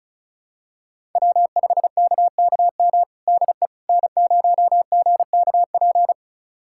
Morse code, heard as a keyed tone, about 36 wpm, 700 Hz. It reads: W5KKM DE N0GKP
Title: USA Callsign DE Callsign - 5 Characters - 35wpm